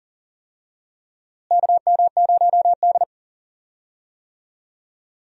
Morse code, heard as a keyed tone, about 40 wpm, 700 Hz. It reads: KM0D
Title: USA Call Signs - 4 Characters - Repeated 3x - 40wpm